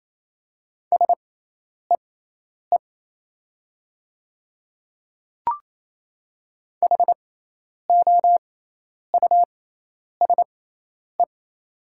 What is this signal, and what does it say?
Morse code, heard as a keyed tone, about 27 wpm, spaced as for 10 wpm, 700 Hz. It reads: SEE HOUSE
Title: 6x Character Spacing - Top 200 Words - 28wpm